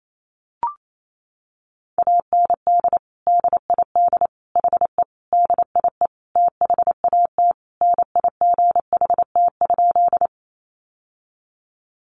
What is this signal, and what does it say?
Morse code, heard as a keyed tone, about 28 wpm, 700 Hz. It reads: AND DID HE DIE THAT NIGHT?
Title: Sentences from Top 1300 Words 28wpm